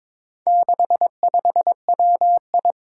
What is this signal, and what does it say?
Morse code, heard as a keyed tone, about 22 wpm, 700 Hz. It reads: 65WI
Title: Four Challenging Letter-Number Combinations 22wpm